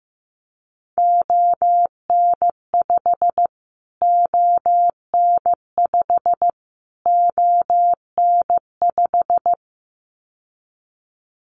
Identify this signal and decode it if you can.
Morse code, heard as a keyed tone, about 15 wpm, 700 Hz. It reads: ON5 ON5 ON5
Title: Callsign Prefixes for Top 25 DXCC - Repeated 3x - 15wpm